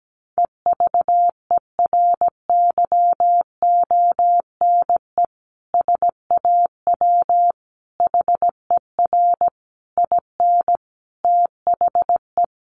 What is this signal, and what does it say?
Morse code, heard as a keyed tone, about 17 wpm, 700 Hz. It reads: EVERYONE SAW HER IN THE
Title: Sentences from Top 800 Words 17wpm